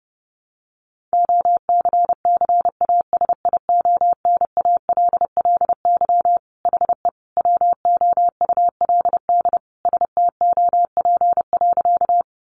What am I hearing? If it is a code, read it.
Morse code, heard as a keyed tone, about 30 wpm, 700 Hz. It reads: OCCASIONALLY HE WOULD STOP.
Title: Sentences from Top 1300 Words 30wpm